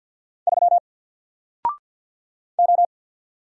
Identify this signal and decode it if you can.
Morse code, heard as a keyed tone, about 48 wpm, 700 Hz. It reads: 3 O
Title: Single Letter-Number Alt 50wpm